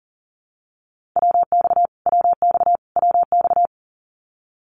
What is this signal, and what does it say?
Morse code, heard as a keyed tone, about 40 wpm, 700 Hz. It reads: WX WX WX
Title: Abbreviations Repeated 3x - 40wpm